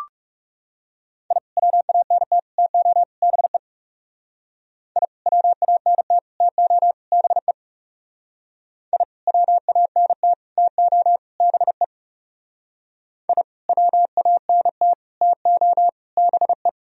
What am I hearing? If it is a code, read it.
Morse code, heard as a keyed tone, about 37 wpm, 700 Hz. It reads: I WANT TO BE I WANT TO BE I WANT TO BE I WANT TO BE
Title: Speed-Racing - Sets of 4 Words 30wpm